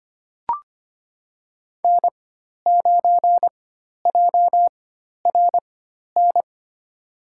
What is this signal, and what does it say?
Morse code, heard as a keyed tone, about 25 wpm, spaced as for 12 wpm, 700 Hz. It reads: N9JRN
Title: USA Call Signs - 5 Characters - 4x Character Spacing - 25wpm